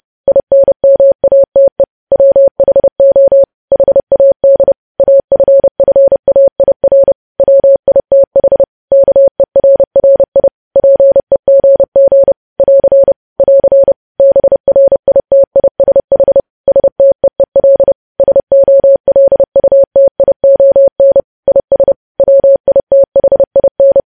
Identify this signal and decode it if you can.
Morse code, heard as a keyed tone, about 30 wpm, 550 Hz. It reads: INMATE WHO HAD AFFAIR WITH KERRI PEGG <AR> <AR> BRITISH STEEL SOLUTION IS WITHIN
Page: news headlines in morse code at 30 wpm